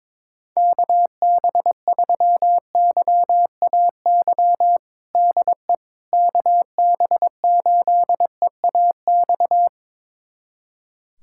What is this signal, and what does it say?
Morse code, heard as a keyed tone, about 21 wpm, 700 Hz. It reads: KB3YAY DE KB8EAX
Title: USA Callsign DE Callsign - 6 Characters - 22wpm